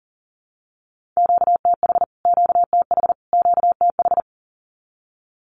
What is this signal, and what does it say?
Morse code, heard as a keyed tone, about 40 wpm, 700 Hz. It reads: QTH QTH QTH
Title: Q-Codes Repeated 3x 40wpm